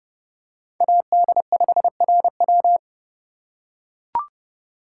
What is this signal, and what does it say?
Morse code, heard as a keyed tone, about 30 wpm, 700 Hz. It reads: AD5RW
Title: USA Call Signs - 5 Characters - 3x Character Spacing - 30wpm